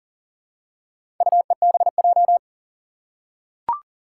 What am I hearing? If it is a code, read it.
Morse code, heard as a keyed tone, about 40 wpm, 700 Hz. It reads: UEBJ
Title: Four Challenging Letter-Number Combinations 40wpm